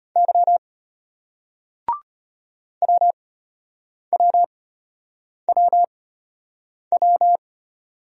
Morse code, heard as a keyed tone, about 35 wpm, spaced as for 8 wpm, 700 Hz. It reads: Y WWWW